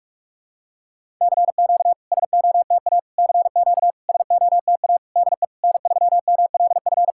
Morse code, heard as a keyed tone, about 44 wpm, 690 Hz. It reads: CQ SOTA CQ SOTA DE N3MLF
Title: SOTA QSO Practice 45wpm